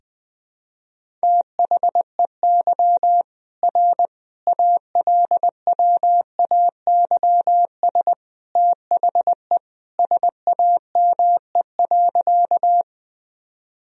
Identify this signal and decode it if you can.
Morse code, heard as a keyed tone, about 20 wpm, 700 Hz. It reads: THEY R ALWAYS THE SAME.
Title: Sentences from Top 300 Words and Abbreviations 20wpm